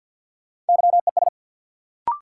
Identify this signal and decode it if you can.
Morse code, heard as a keyed tone, about 50 wpm, 700 Hz. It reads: YES
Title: Sentences from Top 500 Words - New Word Drill - 50wpm